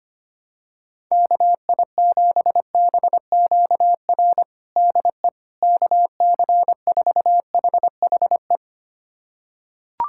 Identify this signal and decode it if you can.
Morse code, heard as a keyed tone, about 25 wpm, 700 Hz. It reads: KI7BQR DE KC4HHE
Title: USA Callsign DE Callsign 25wpm